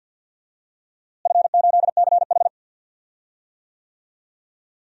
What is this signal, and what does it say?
Morse code, heard as a keyed tone, about 50 wpm, 700 Hz. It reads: F8CH
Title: DXCC-100 Call Signs - 4 Characters - Repeated 2x - 50wpm